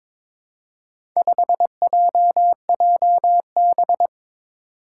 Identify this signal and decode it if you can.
Morse code, heard as a keyed tone, about 22 wpm, 700 Hz. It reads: 5JJB
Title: Four Challenging Letter-Number Combinations 22wpm